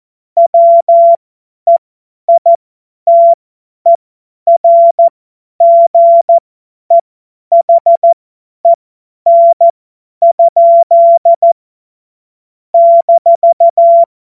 Morse code, heard as a keyed tone, about 14 wpm, spaced as for 10 wpm, 680 Hz. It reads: WEITERGEHEN? -
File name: Morsekurs / Witze / Witz 05 März 2024 50 F70